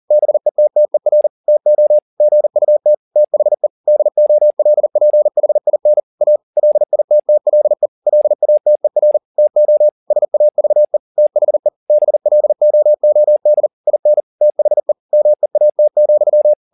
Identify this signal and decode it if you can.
Morse code, heard as a keyed tone, about 40 wpm, 600 Hz. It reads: BETTER TO GUT THE DOLPHIN A LITTLE LATER TO SAVE THE BLOOD IN THE MEAT,
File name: the old man and the sea 2 40wpm